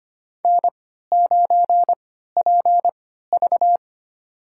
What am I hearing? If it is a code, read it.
Morse code, heard as a keyed tone, about 25 wpm, spaced as for 14 wpm, 700 Hz. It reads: N9PV
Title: USA Call Signs - 4 Characters - 3x Character Spacing - 25wpm